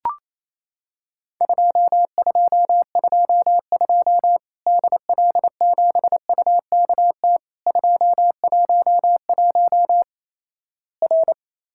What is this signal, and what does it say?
Morse code, heard as a keyed tone, about 28 wpm, 700 Hz. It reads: 2222 DL7UKT 211 R
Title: Worked All Europe DX Contest - QTC Exchange - 28wpm